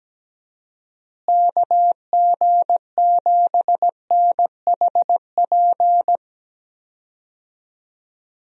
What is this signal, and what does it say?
Morse code, heard as a keyed tone, about 17 wpm, 700 Hz. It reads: KG7NHP
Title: USA Call Signs - 6 Characters - Repeated 3x - 17wpm